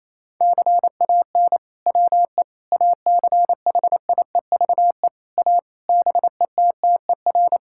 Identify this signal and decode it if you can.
Morse code, heard as a keyed tone, about 28 wpm, 700 Hz. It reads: CAN WE ACHIEVE A BETTER
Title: Sentences from Top 1000 Words - Review Part 6 - 28wpm